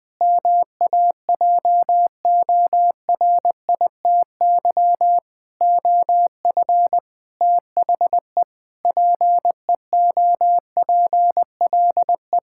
Morse code, heard as a keyed tone, about 20 wpm, 700 Hz. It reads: MAJORITY OF THE PEOPLE
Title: Sentences with 4-Word Phrases - Spoken Beginning - 20wpm